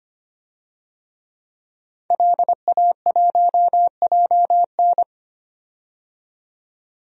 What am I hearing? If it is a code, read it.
Morse code, heard as a keyed tone, about 25 wpm, 700 Hz. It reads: LA1JN
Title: DXCC-100 Call Signs - 5 Characters - 25wpm